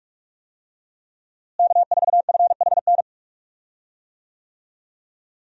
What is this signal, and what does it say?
Morse code, heard as a keyed tone, about 46 wpm, 700 Hz. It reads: K4FHN
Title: USA Call Signs - 5 Characters - Repeated 2x - 45wpm